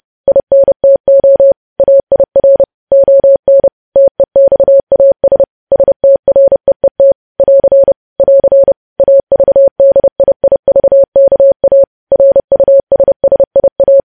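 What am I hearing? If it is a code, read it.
Morse code, heard as a keyed tone, about 30 wpm, 550 Hz. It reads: INTO AIR ON TEXAS STREET <AR> <AR> AVDIIVKA RUSSIA
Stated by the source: Morse code news headlines sent at 30 WPM.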